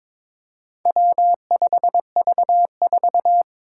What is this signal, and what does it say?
Morse code, heard as a keyed tone, about 22 wpm, 700 Hz. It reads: W5V4
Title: Four Challenging Letter-Number Combinations 22wpm